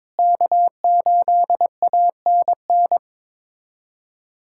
Morse code, heard as a keyed tone, about 22 wpm, 700 Hz. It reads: K8ANN
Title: USA Callsign DE Callsign - 5 Characters - 22wpm